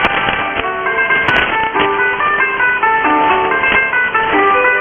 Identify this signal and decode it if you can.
Emergency Alert System (SAME) tones, heard as harsh buzzing bursts. It is heard from 0.9 to 4.1 s.
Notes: Shortwave Pirate Radio Station, Radio Cinco De Mayo. Recorded starting around 00:17 UTC on 05 May 2013 on 6925 kHz USB.